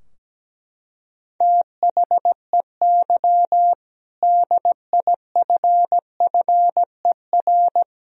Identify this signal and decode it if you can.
Morse code, heard as a keyed tone, about 17 wpm, 700 Hz. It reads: THEY DIFFER
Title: Sentences from Top 1000 Words - Review Part 2 - 17wpm